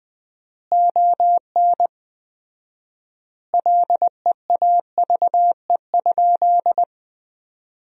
Morse code, heard as a keyed tone, about 21 wpm, 700 Hz. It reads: ON LEAVE?
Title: 4x Word-Spacing - Sentences from Top 400 Words - 20wpm